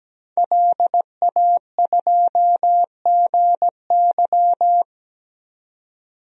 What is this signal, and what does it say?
Morse code, heard as a keyed tone, about 17 wpm, 700 Hz. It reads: LA2GY
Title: DXCC-100 Call Signs - 5 Characters - 17wpm